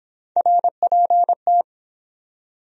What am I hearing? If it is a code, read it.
Morse code, heard as a keyed tone, about 23 wpm, 700 Hz. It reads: RPT